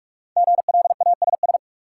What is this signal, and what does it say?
Morse code, heard as a keyed tone, about 42 wpm, 700 Hz. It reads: GLASS
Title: Sentences from Top 1300 Words - New Word Drill Alt - 45wpm